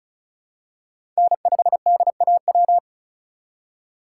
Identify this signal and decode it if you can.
Morse code, heard as a keyed tone, about 36 wpm, 700 Hz. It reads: N5DAW
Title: USA Call Signs - 5 Characters - 4x Character Spacing - 35wpm